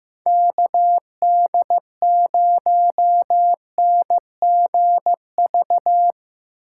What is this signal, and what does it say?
Morse code, heard as a keyed tone, about 17 wpm, 700 Hz. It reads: KD0NGV
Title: Speed Racing - USA 2x3 Call Signs 15wpm